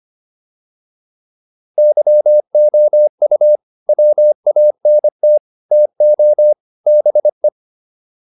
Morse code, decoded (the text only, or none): YOU WANT TO BE